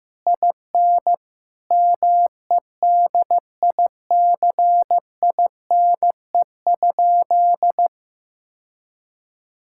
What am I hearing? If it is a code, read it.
Morse code, heard as a keyed tone, about 15 wpm, 700 Hz. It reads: IN MEDICINE?
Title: Sentences from Top 1000 Words - Review Part 6 - 15wpm